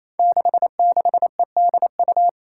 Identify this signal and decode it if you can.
Morse code, heard as a keyed tone, about 29 wpm, 700 Hz. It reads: 66EDU
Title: Five Challenging Letter-Number Combinations 28wpm